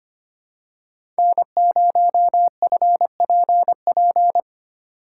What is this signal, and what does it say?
Morse code, heard as a keyed tone, about 25 wpm, 700 Hz. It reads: N0FPP
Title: USA Call Signs - 5 Characters - 4x Character Spacing - 25wpm